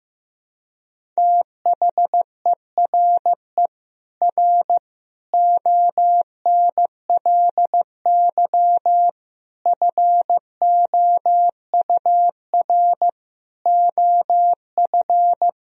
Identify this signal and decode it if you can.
Morse code, heard as a keyed tone, about 15 wpm, 700 Hz. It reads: THERE R ONLY FOUR OF
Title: Sentences from Top 300 Words and Abbreviations 15wpm